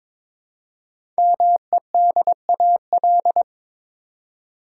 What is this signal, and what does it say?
Morse code, heard as a keyed tone, about 22 wpm, 700 Hz. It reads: MEDAL
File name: 5 Letter Words - First 3 Letters Given - 22wpm